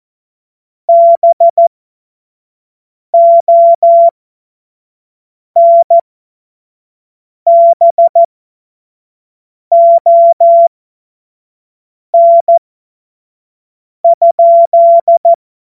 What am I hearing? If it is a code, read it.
Morse code, heard as a keyed tone, about 14 wpm, spaced as for 5 wpm, 680 Hz. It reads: BONBON?
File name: Morsekurs / Witze / Witz 19. Oktober 2021, 25 F70